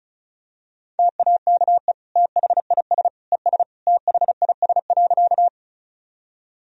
Morse code, heard as a keyed tone, about 34 wpm, 690 Hz. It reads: TAKE THIS ES THIS.
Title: Sentences from Top 200 Words and Abbreviations 35wpm